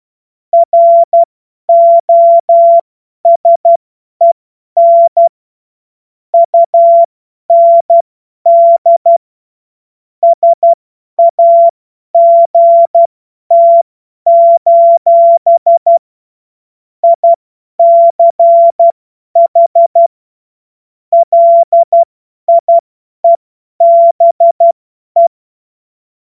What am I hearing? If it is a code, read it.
Morse code, heard as a keyed tone, about 12 wpm, spaced as for 10 wpm, 680 Hz. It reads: ROSEN UND SAGT: ICH LIEBE
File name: Morsekurs / Witze / Witz 07. Juni 2022 50 F70